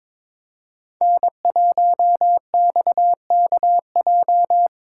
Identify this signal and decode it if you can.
Morse code, heard as a keyed tone, about 25 wpm, 700 Hz. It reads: N1XKJ
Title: Speed Racing - USA 1x3 Call Signs 22wpm